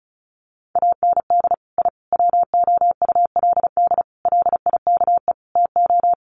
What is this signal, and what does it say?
Morse code, heard as a keyed tone, about 34 wpm, 700 Hz. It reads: AND I WOULD LIKE TO
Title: Sets of 5 Words Encore 35wpm